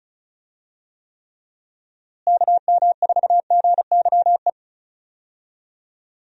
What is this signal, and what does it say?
Morse code, heard as a keyed tone, about 36 wpm, 700 Hz. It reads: KM4GYE
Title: USA Call Signs - 6 Characters - Repeated 3x - 35wpm